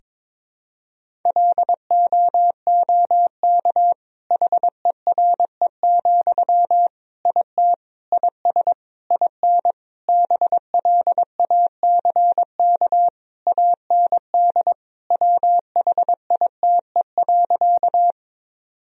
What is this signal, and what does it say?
Morse code, heard as a keyed tone, about 22 wpm, 700 Hz. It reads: LOOK HERE, IT IS IN BLACK AND WHITE.
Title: Sentences from Top 1000 Words - Review Part 2 - 22wpm